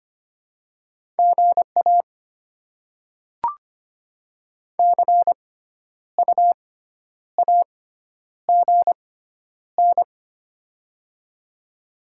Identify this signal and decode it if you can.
Morse code, heard as a keyed tone, about 25 wpm, spaced as for 10 wpm, 700 Hz. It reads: GA CU AGN
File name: Top 20 QSO Elements - 6x Character Spacing - 25wpm